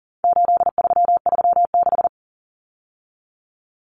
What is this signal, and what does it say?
Morse code, heard as a keyed tone, about 40 wpm, 700 Hz. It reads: 8336
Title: Common Cut Numbers - 4-Digit 40wpm